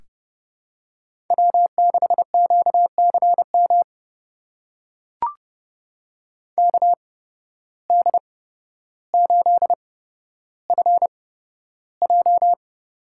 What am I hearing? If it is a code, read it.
Morse code, heard as a keyed tone, about 30 wpm, spaced as for 13 wpm, 700 Hz. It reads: W6QCM KD8FJ
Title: USA Call Signs - 5 Characters - 8x Character Spacing - 30wpm